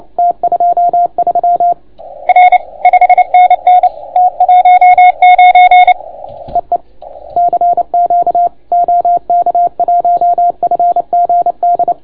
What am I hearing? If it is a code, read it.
Morse code, heard as a keyed tone, about 28 wpm, 690 Hz. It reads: T23 R 5NN T19 EE CQ OK1FGD